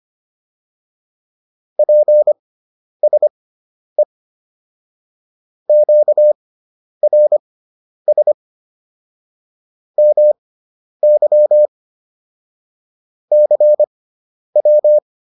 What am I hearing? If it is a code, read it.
Morse code, heard as a keyed tone, about 25 wpm, spaced as for 10 wpm, 600 Hz. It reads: PSE QRS MY CW
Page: intermediate-practice-files